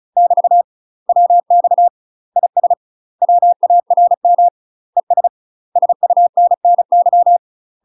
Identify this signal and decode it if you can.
Morse code, heard as a keyed tone, about 36 wpm, 700 Hz. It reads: <BT> WX IS WARM ES SUNNY
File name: UK MASTER QSOs TRAINING/35WPM